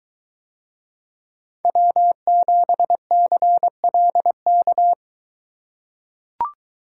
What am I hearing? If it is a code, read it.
Morse code, heard as a keyed tone, about 21 wpm, 700 Hz. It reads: W7CLK